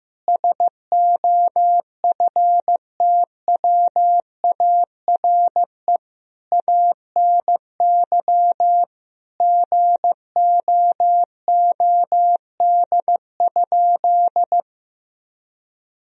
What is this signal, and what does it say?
Morse code, heard as a keyed tone, about 15 wpm, 700 Hz. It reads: SOFTWARE ANY GOOD?
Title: Sentences from Top 1000 Words - Review Part 3 - 15wpm